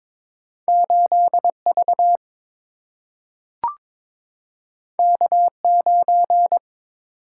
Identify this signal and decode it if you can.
Morse code, heard as a keyed tone, about 22 wpm, 700 Hz. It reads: 8 V K 9
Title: Single-Letter Single-Number 22wpm